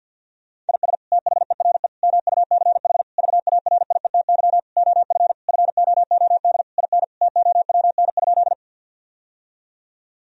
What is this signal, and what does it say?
Morse code, heard as a keyed tone, about 50 wpm, 700 Hz. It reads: IS THERE MUCH VARIETY OF FOOD IN TOWN?
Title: Sentences from Top 1000 Words - Review Part 4 - 50wpm